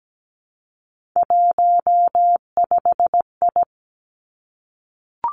Morse code, heard as a keyed tone, about 17 wpm, 700 Hz. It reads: 15I
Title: Three Challenging Letter-Number Combinations 17wpm